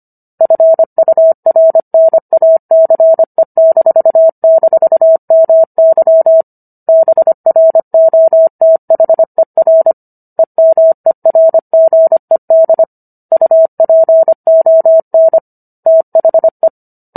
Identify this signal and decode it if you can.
Morse code, heard as a keyed tone, about 25 wpm, 650 Hz. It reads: FURNACE--MY BROTHER EMERGED UPON THE